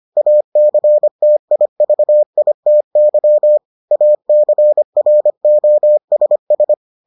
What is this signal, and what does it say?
Morse code, heard as a keyed tone, about 25 wpm, 600 Hz. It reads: ACTIVITY ACROSS